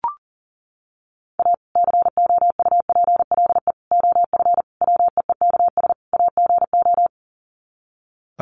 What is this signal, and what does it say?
Morse code, heard as a keyed tone, about 37 wpm, 700 Hz. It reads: A COUPLE OF WEEKS AGO
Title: Sets of 5 Words 40wpm